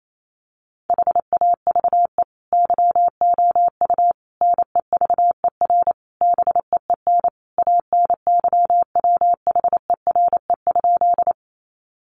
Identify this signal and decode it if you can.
Morse code, heard as a keyed tone, about 28 wpm, 700 Hz. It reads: HAVE YOU NEVER BEEN ANYWHERE?
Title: Sentences from Top 1400 Words 28wpm